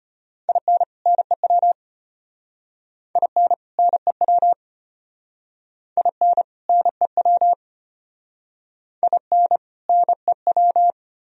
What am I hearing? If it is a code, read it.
Morse code, heard as a keyed tone, about 34 wpm, 700 Hz. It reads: IN NEW IN NEW IN NEW IN NEW